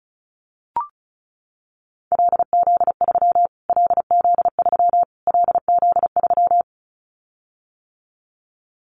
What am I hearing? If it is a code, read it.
Morse code, heard as a keyed tone, about 34 wpm, 700 Hz. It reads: LZ3 LZ3 LZ3
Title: Callsign Prefixes for Top 25 DXCC - Repeated 3x - 35wpm